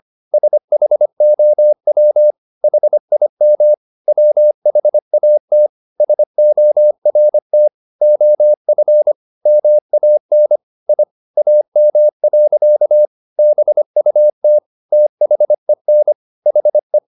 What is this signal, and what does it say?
Morse code, heard as a keyed tone, about 25 wpm, 600 Hz. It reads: SHOW HIM WHAT SORT OF MAN I AM. BUT THEN HE
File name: the old man and the sea part 1 25wpm